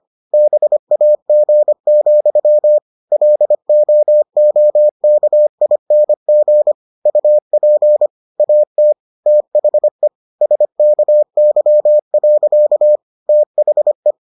Morse code, heard as a keyed tone, about 25 wpm, 600 Hz. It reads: BAG, LOOKING UP AT THE SKY. THE